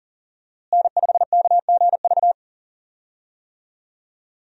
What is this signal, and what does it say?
Morse code, heard as a keyed tone, about 40 wpm, 700 Hz. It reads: N5KGV
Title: USA Call Signs - 5 Characters - Repeated 2x - 40wpm